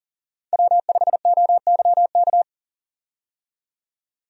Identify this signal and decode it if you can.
Morse code, heard as a keyed tone, about 40 wpm, 700 Hz. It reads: W5OYK
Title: USA Call Signs - 5 Characters - 40wpm